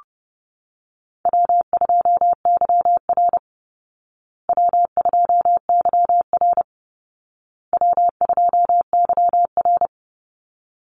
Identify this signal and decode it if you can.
Morse code, heard as a keyed tone, about 30 wpm, 700 Hz. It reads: W2YR W2YR W2YR
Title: Call Signs Repeated 2x - 30wpm